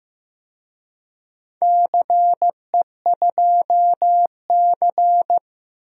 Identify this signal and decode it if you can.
Morse code, heard as a keyed tone, about 16 wpm, 700 Hz. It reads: CE2C
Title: DXCC-100 Call Signs - 4 Characters - Repeated 2x - 15wpm